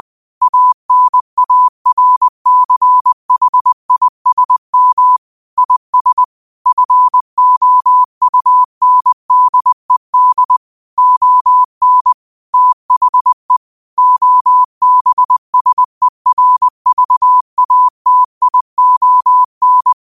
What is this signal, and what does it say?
Morse code, heard as a keyed tone, about 20 wpm, 1000 Hz. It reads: ANARCHISM IS FOUNDED ON THE OBSERVATION